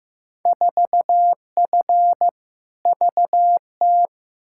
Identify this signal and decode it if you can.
Morse code, heard as a keyed tone, about 15 wpm, 700 Hz. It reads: 4F VT
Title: Field Day Exchanges Encore - 15wpm